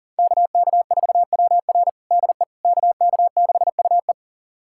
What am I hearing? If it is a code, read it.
Morse code, heard as a keyed tone, about 37 wpm, 700 Hz. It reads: KK4WR DE KK6UE
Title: USA Callsign DE Callsign - 5 Characters - 40wpm